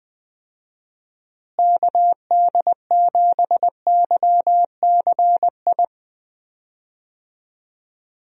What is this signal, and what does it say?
Morse code, heard as a keyed tone, about 20 wpm, 700 Hz. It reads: KD7YCI
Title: USA Call Signs - 6 Characters - Repeated 2x - 20wpm